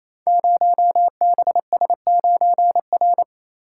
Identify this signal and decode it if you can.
Morse code, heard as a keyed tone, about 28 wpm, 700 Hz. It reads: 0BS9R
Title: Code Groups - 5 Letter:Numbers - 28wpm